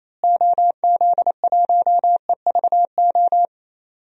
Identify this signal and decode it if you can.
Morse code, heard as a keyed tone, about 27 wpm, 700 Hz. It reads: OZ1EVO
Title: DXCC-100 Call Signs - 6 Characters - Repeated 2x - 28wpm